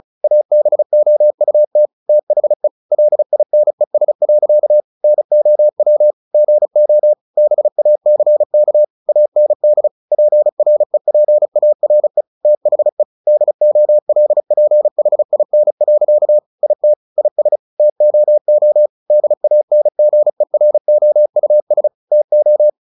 Morse code, heard as a keyed tone, about 34 wpm, 600 Hz. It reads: ABOUT THE LINES. NOW GO BACK AND PREPARE THE DOLPHIN. IT IS TOO DANGEROUS TO
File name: the old man and the sea 2 35wpm